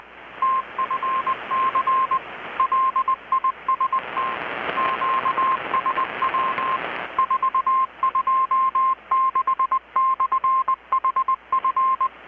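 Morse code, heard as a keyed tone, about 20 wpm, 1060 Hz, with radio noise. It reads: TFC LIST QSW 426/HF
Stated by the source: Morse from KPH's Night of Nights XX maritime radiotelegraphy event, received in Chiapas, Mexico about 3,800 km away on a Kenwood R-600 with a grounded 10 m long-wire antenna.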